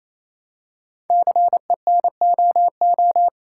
Morse code, heard as a keyed tone, about 27 wpm, 700 Hz. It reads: CENOO
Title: Code Groups - 5 Letters - 28wpm